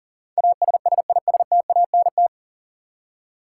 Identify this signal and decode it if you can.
Morse code, heard as a keyed tone, about 40 wpm, 700 Hz. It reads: ASSISTANT
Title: Practice QSO - Exchange 3 - More Info - 40wpm